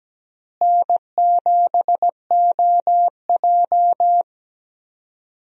Morse code, heard as a keyed tone, about 17 wpm, 700 Hz. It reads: N7OJ
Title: USA Call Signs - 4 Characters - Repeated 2x - 17wpm